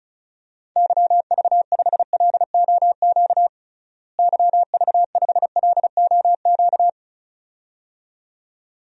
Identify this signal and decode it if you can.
Morse code, heard as a keyed tone, about 36 wpm, 700 Hz. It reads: YV5LOQ YV5LOQ
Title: DXCC-100 Call Signs - 6 Characters - Repeated 2x - 35wpm